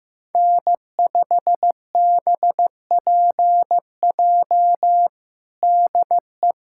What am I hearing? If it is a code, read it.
Morse code, heard as a keyed tone, about 15 wpm, 700 Hz. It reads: N5BPJ DE
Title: K1USN SST QSO Practice 15wpm